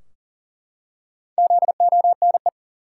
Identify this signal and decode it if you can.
Morse code, heard as a keyed tone, about 40 wpm, 700 Hz. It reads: ZONE
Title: Four Letter Words - 40wpm